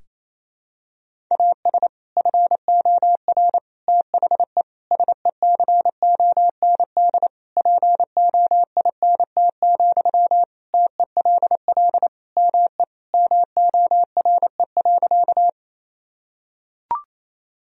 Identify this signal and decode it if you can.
Morse code, heard as a keyed tone, about 28 wpm, 700 Hz. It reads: AS FOR THE SECOND POINT, TELL ME MORE.
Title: Sentences from Top 1000 Words - Review Part 1 - 28wpm